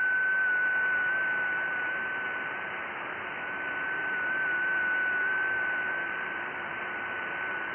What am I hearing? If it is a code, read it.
FT8.
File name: IC-7100